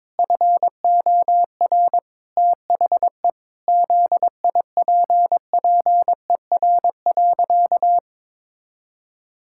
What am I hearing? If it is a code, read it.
Morse code, heard as a keyed tone, about 22 wpm, 700 Hz. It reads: FOR THE ZIPPER.
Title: Pangram Challenge v2 - 22wpm